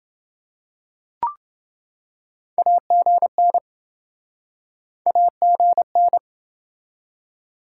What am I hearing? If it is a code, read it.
Morse code, heard as a keyed tone, about 27 wpm, 700 Hz. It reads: AGN AGN